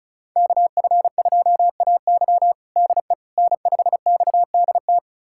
Morse code, heard as a keyed tone, about 36 wpm, 700 Hz. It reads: KF2AY DE N5XDT
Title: USA Callsign DE Callsign - 5 Characters - 35wpm